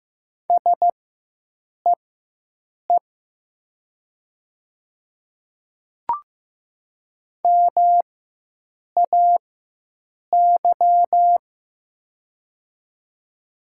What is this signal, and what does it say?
Morse code, heard as a keyed tone, about 15 wpm, spaced as for 7 wpm, 700 Hz. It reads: SEE MAY
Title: 4x Character Spacing - Top 100 Words - 15wpm